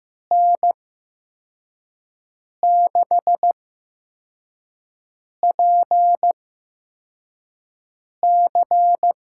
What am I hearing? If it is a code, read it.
Morse code, heard as a keyed tone, about 15 wpm, spaced as for 4 wpm, 700 Hz. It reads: N6PC
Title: USA Call Signs - 8x Character Spacing - 15wpm